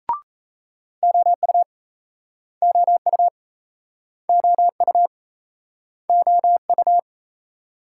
Morse code, heard as a keyed tone, about 39 wpm, 700 Hz. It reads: OU OU OU OU